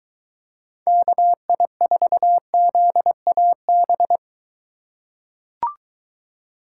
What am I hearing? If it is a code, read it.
Morse code, heard as a keyed tone, about 21 wpm, 700 Hz. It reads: KI4ZAB